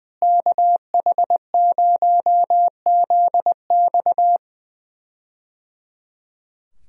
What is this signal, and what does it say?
Morse code, heard as a keyed tone, about 20 wpm, 700 Hz. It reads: KH0ZX
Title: USA Call Signs - 5 Characters - Repeated 3x - 20wpm